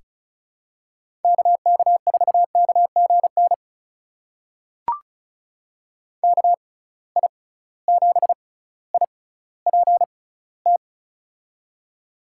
Morse code, heard as a keyed tone, about 34 wpm, spaced as for 18 wpm, 700 Hz. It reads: KK4KGN KI7IPT
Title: USA Call Signs - 6 Characters - 6x Character Spacing - 35wpm